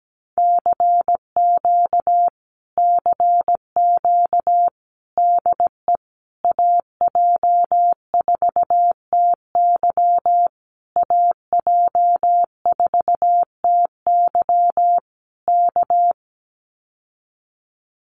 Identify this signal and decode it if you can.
Morse code, heard as a keyed tone, about 17 wpm, 700 Hz. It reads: CQ CQ DE AJ4TY AJ4TY K
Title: Common QSO Phrases Encore 17wpm